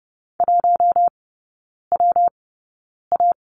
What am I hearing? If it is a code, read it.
Morse code, heard as a keyed tone, about 30 wpm, spaced as for 9 wpm, 700 Hz. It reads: 1WA
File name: Lesson 19e - Letter w - Three Characters 30wpm